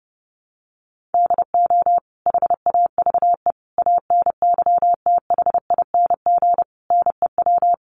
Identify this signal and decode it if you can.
Morse code, heard as a keyed tone, about 30 wpm, 700 Hz. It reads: DO HAVE ANYTHING NEW
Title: Sentences from Top 1300 Words 30wpm